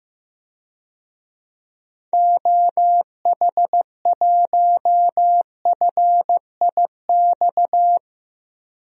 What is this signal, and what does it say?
Morse code, heard as a keyed tone, about 15 wpm, 700 Hz. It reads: OH1FIX
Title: DXCC-100 Call Signs - 6 Characters - 15wpm